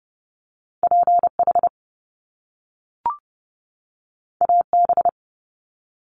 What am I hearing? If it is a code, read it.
Morse code, heard as a keyed tone, about 30 wpm, 700 Hz. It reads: PH AB
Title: Top Two Letter Combinations in Words 30wpm